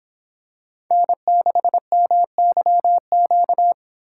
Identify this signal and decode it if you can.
Morse code, heard as a keyed tone, about 25 wpm, 700 Hz. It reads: N6MYQ